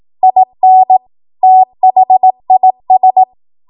Morse code, heard as a keyed tone, about 18 wpm, 750 Hz. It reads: IN THIS
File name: Morse / 18WPM